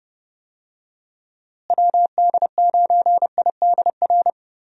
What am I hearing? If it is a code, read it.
Morse code, heard as a keyed tone, about 30 wpm, 700 Hz. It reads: WD9IDR
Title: USA Call Signs - 6 Characters - Repeated 2x - 30wpm